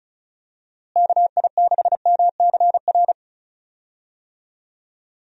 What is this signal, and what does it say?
Morse code, heard as a keyed tone, about 36 wpm, 700 Hz. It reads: KI6MCR
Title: USA Call Signs - 6 Characters - Repeated 2x - 35wpm